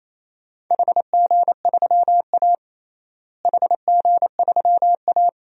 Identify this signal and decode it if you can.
Morse code, heard as a keyed tone, about 27 wpm, 700 Hz. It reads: HG3A HG3A
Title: DXCC-100 Call Signs - 4 Characters - Repeated 3x - 28wpm